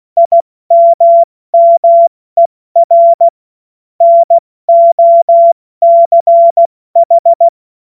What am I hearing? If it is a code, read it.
Morse code, heard as a keyed tone, about 16 wpm, 680 Hz. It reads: IMMER NOCH
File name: Morsekurs / Witze / Witz 08. August 2023 70 F80